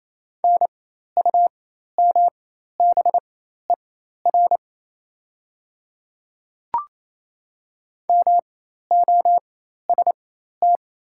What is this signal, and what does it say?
Morse code, heard as a keyed tone, about 28 wpm, spaced as for 13 wpm, 690 Hz. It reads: NUMBER MOST
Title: 4x Character Spacing - Top 200 Words - 28wpm